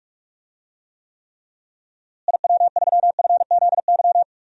Morse code, heard as a keyed tone, about 46 wpm, 700 Hz. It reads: IW3FZY
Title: DXCC-100 Call Signs - 6 Characters - Repeated 3x - 45wpm